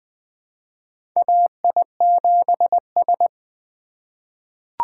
Morse code, heard as a keyed tone, about 20 wpm, 700 Hz. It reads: AI7S
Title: USA Call Signs - 4 Characters - 3x Character Spacing - 20wpm